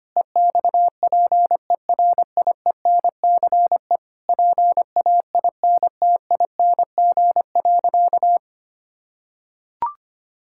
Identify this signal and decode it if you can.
Morse code, heard as a keyed tone, about 25 wpm, 700 Hz. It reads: EXPERIENCE PAINTING.
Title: Sentences from Top 1000 Words - Review Part 6 - 25wpm